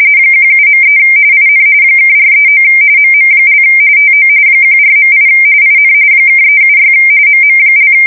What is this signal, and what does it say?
RTTY, about 45.45 baud. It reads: 17.3 IN (439 MM) TO 14.8 IN (376 MM)
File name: fsk170 / USS-Missouri